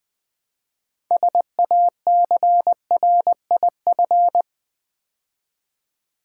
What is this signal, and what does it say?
Morse code, heard as a keyed tone, about 20 wpm, 700 Hz. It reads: SACRIF
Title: Build A Word - 3 Letters At A Time - 9-Letter Words - 20wpm